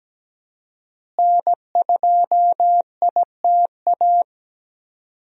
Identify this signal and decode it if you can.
Morse code, heard as a keyed tone, about 17 wpm, 700 Hz. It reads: N2ITA
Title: USA Call Signs - 5 Characters - Repeated 3x - 17wpm